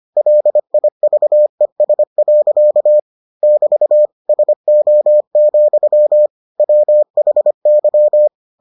Morse code, heard as a keyed tone, about 25 wpm, 600 Hz. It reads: LIVES. <BT> SO, WHY